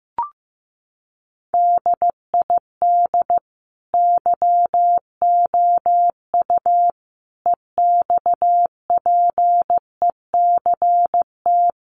Morse code, heard as a keyed tone, about 14 wpm, 700 Hz. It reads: DID YOU EXPECT
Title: Sentences from Top 1100 Words 15wpm